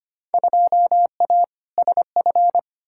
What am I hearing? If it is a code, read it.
Morse code, heard as a keyed tone, about 25 wpm, 700 Hz. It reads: 2A SF
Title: Field Day Exchanges - 25wpm